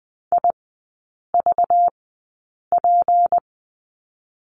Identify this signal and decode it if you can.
Morse code, heard as a keyed tone, about 20 wpm, spaced as for 8 wpm, 690 Hz. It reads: IVP
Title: 191 Lesson 31e - Letter v - Three Characters 20wpm